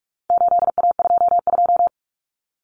Morse code, heard as a keyed tone, about 44 wpm, 700 Hz. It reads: 8A22
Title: Common Cut Numbers - 4-Digit 45wpm